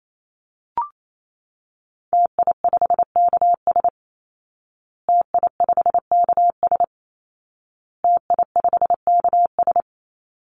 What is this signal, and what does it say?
Morse code, heard as a keyed tone, about 28 wpm, 700 Hz. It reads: TI5KS TI5KS TI5KS
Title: Call Signs Repeated 2x - 28wpm